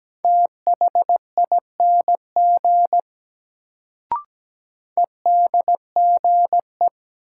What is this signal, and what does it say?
Morse code, heard as a keyed tone, about 17 wpm, 700 Hz. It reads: THING EDGE
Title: Sets of 3 Words - New Word Drill - 17wpm